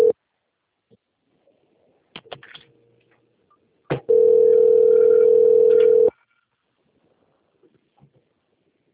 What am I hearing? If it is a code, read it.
Ringback tone (North America), 440 + 480 Hz.